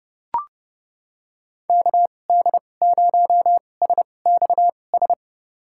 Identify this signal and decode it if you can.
Morse code, heard as a keyed tone, about 30 wpm, spaced as for 22 wpm, 700 Hz. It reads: KD0SXS
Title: USA Call Signs - 6 Characters - 2x Character Spacing - 30wpm